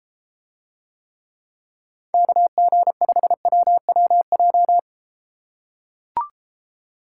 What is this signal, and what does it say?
Morse code, heard as a keyed tone, about 34 wpm, 700 Hz. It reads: KG5WWJ